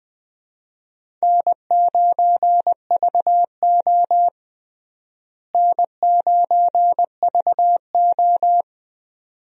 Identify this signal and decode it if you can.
Morse code, heard as a keyed tone, about 20 wpm, 700 Hz. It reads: N9VO N9VO
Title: USA Call Signs - 4 Characters - Repeated 2x - 20wpm